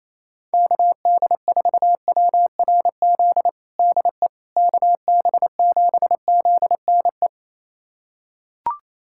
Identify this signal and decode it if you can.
Morse code, heard as a keyed tone, about 28 wpm, 700 Hz. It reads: KD4WRZ DE KB7ZNE
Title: USA Callsign DE Callsign - 6 Characters - 28wpm